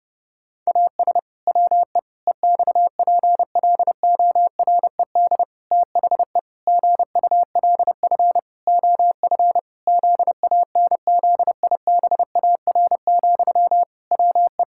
Morse code, heard as a keyed tone, about 30 wpm, 700 Hz. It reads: AS WE EXPLORED THE GULF OF ZANZIBAR, WE
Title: Pangram Challenge v2 - 30wpm